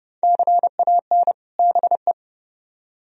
Morse code, heard as a keyed tone, about 36 wpm, 700 Hz. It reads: CAN BE
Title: Speed-Racing - Sets of 2 Words 30wpm